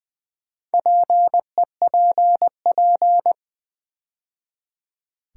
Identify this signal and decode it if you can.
Morse code, heard as a keyed tone, about 20 wpm, 690 Hz. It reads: PEPP
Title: Build A Word - 2 Letters At A Time - 6-Letter Words - 20wpm